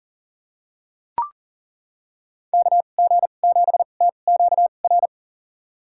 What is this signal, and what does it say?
Morse code, heard as a keyed tone, about 39 wpm, spaced as for 29 wpm, 700 Hz. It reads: KG7TQR
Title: USA Call Signs - 2x Character Spacing - 40wpm